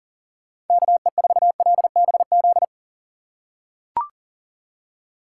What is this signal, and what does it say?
Morse code, heard as a keyed tone, about 40 wpm, 700 Hz. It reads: KE4LBZ
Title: USA Call Signs - 6 Characters - 4x Character Spacing - 40wpm